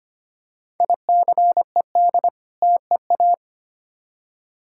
Morse code, heard as a keyed tone, about 25 wpm, 700 Hz. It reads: ICED TEA
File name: practice / 25wpm / Difficult Words with Hints - 25wpm